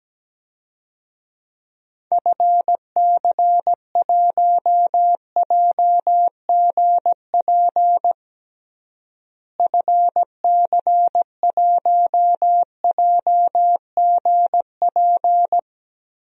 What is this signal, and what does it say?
Morse code, heard as a keyed tone, about 17 wpm, 700 Hz. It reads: FC1JGP FC1JGP
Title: DXCC-100 Call Signs - 6 Characters - Repeated 3x - 17wpm